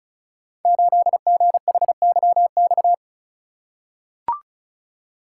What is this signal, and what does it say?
Morse code, heard as a keyed tone, about 36 wpm, 700 Hz. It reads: 8GHYX
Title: Code Groups - 5 Letter:Numbers - 35wpm